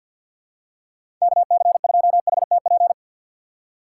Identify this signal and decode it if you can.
Morse code, heard as a keyed tone, about 53 wpm, 700 Hz. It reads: KC2HTP
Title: USA Call Signs - 6 Characters - 4x Character Spacing - 50wpm